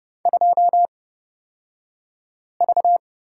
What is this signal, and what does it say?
Morse code, heard as a keyed tone, about 30 wpm, spaced as for 5 wpm, 700 Hz. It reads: 2V
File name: Lesson 36d - BT - Two Characters 30wpm